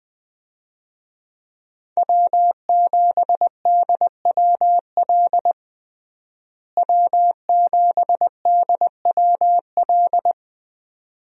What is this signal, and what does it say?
Morse code, heard as a keyed tone, about 20 wpm, 700 Hz. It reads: W7DWL W7DWL
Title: USA Call Signs - 5 Characters - Repeated 2x - 20wpm